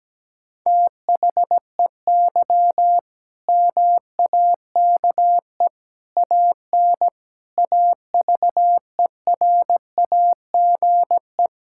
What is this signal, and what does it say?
Morse code, heard as a keyed tone, about 17 wpm, 700 Hz. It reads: THEY MAKE AN AVERAGE
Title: Sentences from Top 1000 Words - Review Part 6 - 17wpm